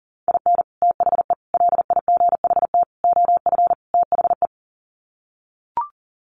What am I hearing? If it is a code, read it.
Morse code, heard as a keyed tone, about 40 wpm, 700 Hz. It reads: IN THE LIGHT OF THE
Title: Sets of 5 Words Encore2 40wpm